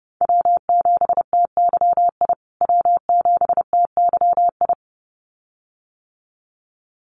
Morse code, heard as a keyed tone, about 30 wpm, 700 Hz. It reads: W7TYI W7TYI
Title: USA Call Signs Repeated 2x 30wpm